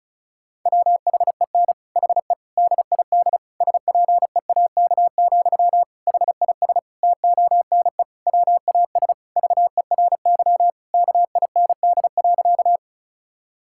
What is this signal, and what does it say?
Morse code, heard as a keyed tone, about 36 wpm, 700 Hz. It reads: WHEN HE DID SPEAK, HIS TONE WAS VERY KIND.
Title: Sentences from Top 1600 Words 35wpm